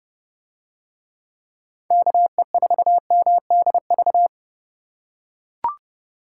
Morse code, heard as a keyed tone, about 30 wpm, 700 Hz. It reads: KE4MDV